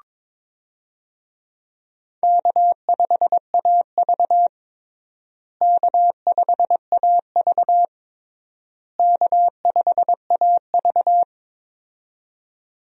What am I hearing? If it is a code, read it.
Morse code, heard as a keyed tone, about 22 wpm, 700 Hz. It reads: K5AV K5AV K5AV
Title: USA Call Signs - 4 Characters - Repeated 3x - 22wpm